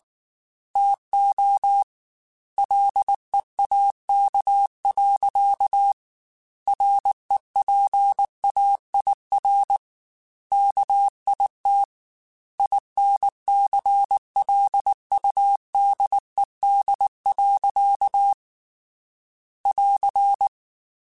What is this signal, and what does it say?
Morse code, heard as a keyed tone, about 19 wpm, 800 Hz. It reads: TO LEAK. REPAIR KIT INCLUDED. <AR>